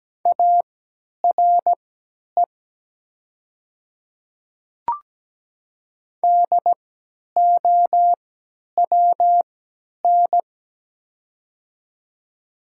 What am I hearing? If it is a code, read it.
Morse code, heard as a keyed tone, about 17 wpm, spaced as for 10 wpm, 700 Hz. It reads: ARE DOWN
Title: 3x Character Spacing - Top 100 Words - 17wpm